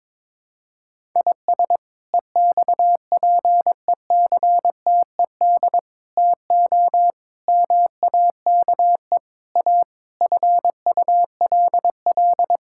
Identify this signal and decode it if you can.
Morse code, heard as a keyed tone, about 22 wpm, 700 Hz. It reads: IS EXPECTED TO MAKE A FULL
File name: practice / 22wpm / Sets of 7 Words - 22wpm